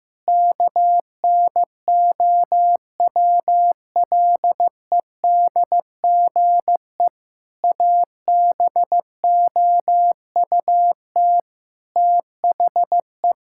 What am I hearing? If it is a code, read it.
Morse code, heard as a keyed tone, about 15 wpm, 700 Hz. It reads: KNOWLEDGE ABOUT THE
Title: Sentences from Top 1000 Words - Review Part 6 - 15wpm